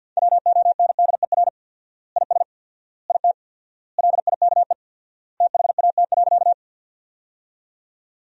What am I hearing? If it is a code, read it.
Morse code, heard as a keyed tone, about 48 wpm, 700 Hz. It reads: WONDER IS IT LIKE THAT.
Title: 4x Word-Spacing - Sentences from Top 500 Words - 50wpm